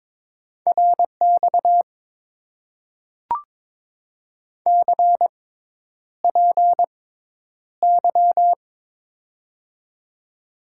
Morse code, heard as a keyed tone, about 22 wpm, spaced as for 10 wpm, 700 Hz. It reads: RX CPY
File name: Top 90 QSO Elements - 6x Character Spacing - 22wpm